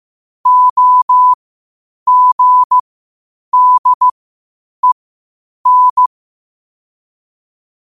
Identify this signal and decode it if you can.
Morse code, heard as a keyed tone, about 15 wpm, spaced as for 8 wpm, 1000 Hz. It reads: OGDEN
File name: qotdfiles / 8wpm